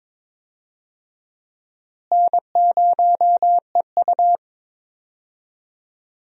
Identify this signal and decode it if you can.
Morse code, heard as a keyed tone, about 21 wpm, 700 Hz. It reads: N0EU
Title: USA Call Signs - 4 Characters - Repeated 3x - 22wpm